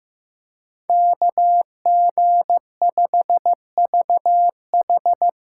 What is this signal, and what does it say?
Morse code, heard as a keyed tone, about 15 wpm, 700 Hz. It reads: KG5VH
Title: USA Call Signs - 6x Character Spacing - 15wpm